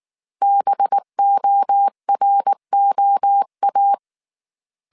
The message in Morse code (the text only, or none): BOROA